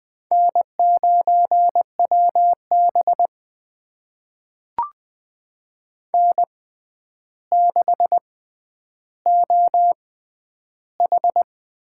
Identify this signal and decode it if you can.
Morse code, heard as a keyed tone, about 20 wpm, spaced as for 10 wpm, 700 Hz. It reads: N9WB N6OH
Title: USA Call Signs - 4 Characters - 6x Character Spacing - 20wpm